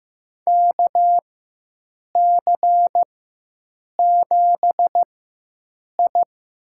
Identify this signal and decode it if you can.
Morse code, heard as a keyed tone, about 15 wpm, spaced as for 7 wpm, 700 Hz. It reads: KC7I
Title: USA Call Signs - 6 Characters - 4x Character Spacing - 15wpm